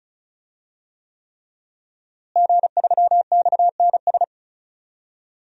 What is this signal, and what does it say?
Morse code, heard as a keyed tone, about 36 wpm, 700 Hz. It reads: G3XNS
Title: DXCC-100 Call Signs - 5 Characters - Repeated 3x - 35wpm